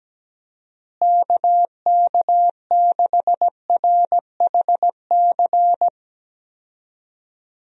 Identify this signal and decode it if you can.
Morse code, heard as a keyed tone, about 17 wpm, 700 Hz. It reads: KK6RHC
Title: USA Call Signs - 6 Characters - Repeated 3x - 17wpm